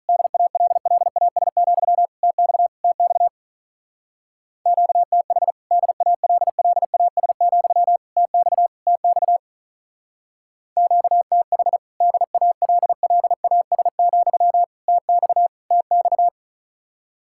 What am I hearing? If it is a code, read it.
Morse code, heard as a keyed tone, about 44 wpm, 700 Hz. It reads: DALLAS, TX TX QTH DALLAS, TX TX QTH DALLAS, TX TX